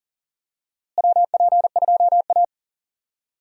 Morse code, heard as a keyed tone, about 37 wpm, 700 Hz. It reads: WP2A
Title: USA Call Signs - 4 Characters - Repeated 3x - 40wpm